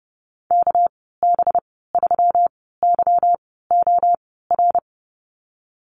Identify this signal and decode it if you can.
Morse code, heard as a keyed tone, about 30 wpm, spaced as for 17 wpm, 700 Hz. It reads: KB3YOR
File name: Lesson 37g - Letter x - Call Signs 30wpm